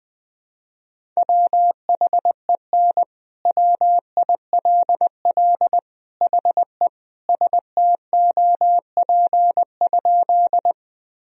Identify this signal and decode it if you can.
Morse code, heard as a keyed tone, about 20 wpm, 700 Hz. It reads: WHEN WILL HE STOP?
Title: Sentences from Top 1000 Words - Review Part 1 - 20wpm